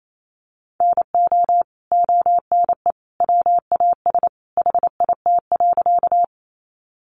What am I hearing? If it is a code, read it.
Morse code, heard as a keyed tone, about 28 wpm, 700 Hz. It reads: NO ONE WAS HIT.
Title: Sentences from Top 1200 Words 28wpm